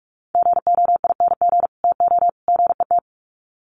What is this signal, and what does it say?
Morse code, heard as a keyed tone, about 44 wpm, 700 Hz. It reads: GOING TO GET
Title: Sets of 3 Words 45wpm